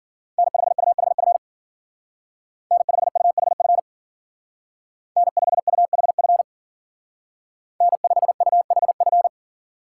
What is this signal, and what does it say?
Morse code, heard as a keyed tone, about 48 wpm, 700 Hz. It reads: N5UHF N5UHF N5UHF N5UHF
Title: Speed Racing - USA 1x3 Call Signs 40wpm